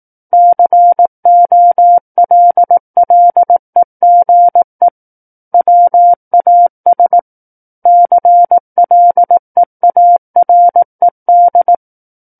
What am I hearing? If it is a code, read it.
Morse code, heard as a keyed tone, about 18 wpm, 700 Hz. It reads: COLLEGE WAS CLEARED
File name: 18 WPM